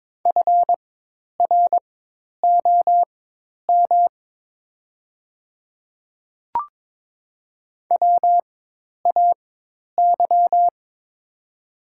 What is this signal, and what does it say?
Morse code, heard as a keyed tone, about 22 wpm, spaced as for 10 wpm, 700 Hz. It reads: FROM WAY
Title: 4x Character Spacing - Top 100 Words - 22wpm